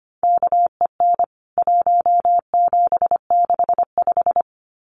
Morse code, heard as a keyed tone, about 25 wpm, 700 Hz. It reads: KEN 1765
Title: CWT Exchange 25wpm